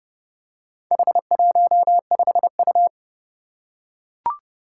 Morse code, heard as a keyed tone, about 28 wpm, 700 Hz. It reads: H15U
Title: Four Challenging Letter-Number Combinations 30wpm